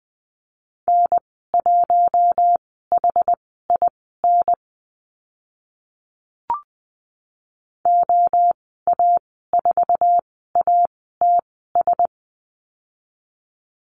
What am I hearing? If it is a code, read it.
Morse code, heard as a keyed tone, about 20 wpm, spaced as for 14 wpm, 700 Hz. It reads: N1HIN OA4ATS
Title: 069 Lesson 12g - Letter d - Call Signs 20wpm